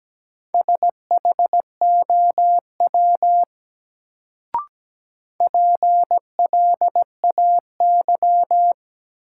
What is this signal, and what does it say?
Morse code, heard as a keyed tone, about 17 wpm, 700 Hz. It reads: SHOW PLAY
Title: Sentences from Top 200 Words - New Word Drill - 17wpm